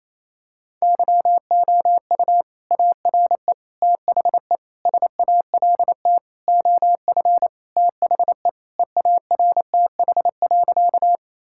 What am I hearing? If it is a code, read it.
Morse code, heard as a keyed tone, about 28 wpm, 700 Hz. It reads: YOU ARE THE SALT OF THE EARTH.
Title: Sentences from Top 1000 Words - Review Part 5 - 28wpm